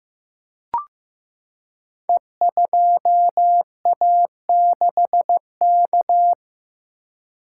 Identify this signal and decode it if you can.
Morse code, heard as a keyed tone, about 15 wpm, 700 Hz. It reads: E2A6K
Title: Code Groups - 5 Letter:Numbers - 15wpm